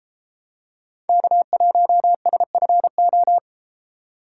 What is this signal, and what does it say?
Morse code, heard as a keyed tone, about 31 wpm, 700 Hz. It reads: K1SFO